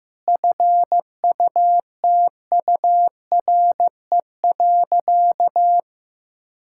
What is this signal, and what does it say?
Morse code, heard as a keyed tone, about 15 wpm, 700 Hz. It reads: FUTURE.
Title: Sentences from Top 1600 Words 15wpm